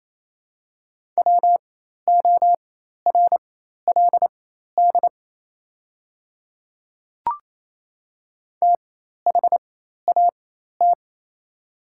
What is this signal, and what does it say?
Morse code, heard as a keyed tone, about 27 wpm, spaced as for 13 wpm, 690 Hz. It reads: WORLD THAT
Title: 4x Character Spacing - Top 200 Words - 28wpm